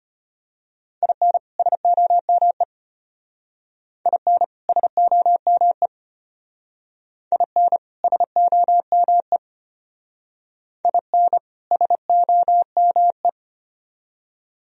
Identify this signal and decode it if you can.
Morse code, heard as a keyed tone, about 31 wpm, 700 Hz. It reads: IN SOME IN SOME IN SOME IN SOME
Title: Speed-Racing - Sets of 2 Words 25wpm